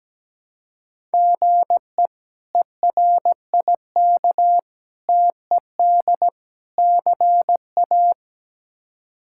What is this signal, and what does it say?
Morse code, heard as a keyed tone, about 17 wpm, 700 Hz. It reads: GE ERIK TED CA
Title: K1USN SST QSO Practice - 17wpm